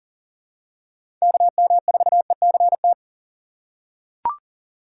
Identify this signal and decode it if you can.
Morse code, heard as a keyed tone, about 40 wpm, 700 Hz. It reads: KM4ECT
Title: USA Call Signs - 6 Characters - 2x Character Spacing - 40wpm